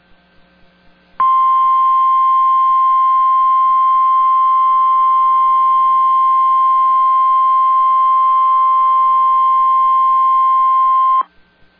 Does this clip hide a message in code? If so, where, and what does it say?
Yes: Emergency Alert System (SAME) tones (the 1050 Hz attention tone) from 1.2 to 11.3 s.